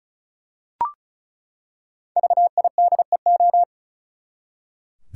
Morse code, heard as a keyed tone, about 36 wpm, 700 Hz. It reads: VIDEO
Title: Five Letter Words - 35wpm